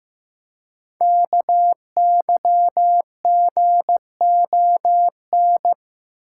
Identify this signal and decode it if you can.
Morse code, heard as a keyed tone, about 15 wpm, 700 Hz. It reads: KYGON
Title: Code Groups - 5 Letters - 15wpm